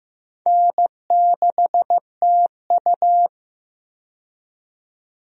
Morse code, heard as a keyed tone, about 15 wpm, 700 Hz. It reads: N6TU
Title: USA Call Signs - 4 Characters - 15wpm